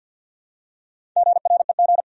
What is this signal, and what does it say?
Morse code, heard as a keyed tone, about 52 wpm, 700 Hz. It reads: GREG
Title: USA Common Names - Max 5 Letters - 50wpm